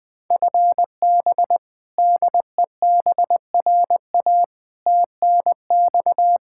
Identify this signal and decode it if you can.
Morse code, heard as a keyed tone, about 20 wpm, 700 Hz. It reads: FB DEBRA TNX
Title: Practice QSO - Exchange 3 - More Info - 20wpm